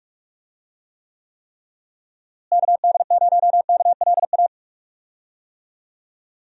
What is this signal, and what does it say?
Morse code, heard as a keyed tone, about 46 wpm, 700 Hz. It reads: KD0KLA
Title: USA Call Signs - 6 Characters - Repeated 3x - 45wpm